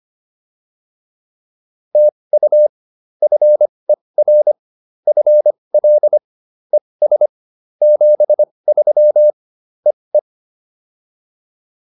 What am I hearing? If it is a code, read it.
Morse code, heard as a keyed tone, about 25 wpm, spaced as for 20 wpm, 600 Hz. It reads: TU FER FL ES 73 EE